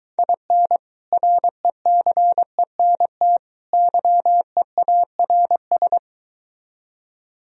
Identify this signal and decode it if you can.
Morse code, heard as a keyed tone, about 23 wpm, 700 Hz. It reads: IN RECENT YEARS